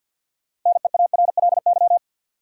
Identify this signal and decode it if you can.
Morse code, heard as a keyed tone, about 48 wpm, 700 Hz. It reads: NEARLY
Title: Sentences from Top 1300 Words - New Word Drill - 50wpm